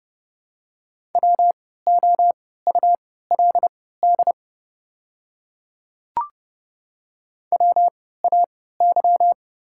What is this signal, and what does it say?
Morse code, heard as a keyed tone, about 30 wpm, spaced as for 17 wpm, 700 Hz. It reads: WOULD WAY
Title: 3x Character Spacing - Top 100 Words - 30wpm